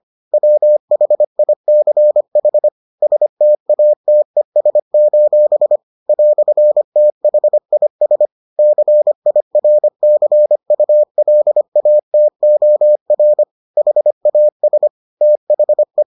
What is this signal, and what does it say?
Morse code, heard as a keyed tone, about 25 wpm, 600 Hz. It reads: WHICH STATES: "THIS CIRCULATOR HAS THE